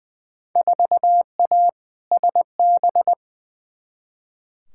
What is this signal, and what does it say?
Morse code, heard as a keyed tone, about 21 wpm, 700 Hz. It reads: 4A SB
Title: Field Day Exchanges Encore - 20wpm